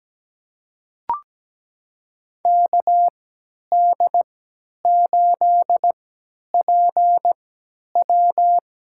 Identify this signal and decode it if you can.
Morse code, heard as a keyed tone, about 17 wpm, spaced as for 10 wpm, 700 Hz. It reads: KD8PW
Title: USA Call Signs - 3x Character Spacing - 17wpm